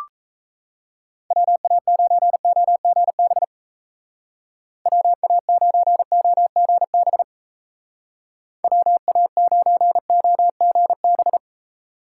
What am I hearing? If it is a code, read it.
Morse code, heard as a keyed tone, about 37 wpm, 700 Hz. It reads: WA9OGB WA9OGB WA9OGB